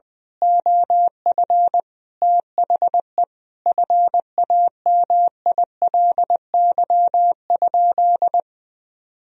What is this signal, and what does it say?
Morse code, heard as a keyed tone, about 20 wpm, 700 Hz. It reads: OF THE FAMILY?
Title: Sentences from Top 1000 Words - Review Part 4 - 20wpm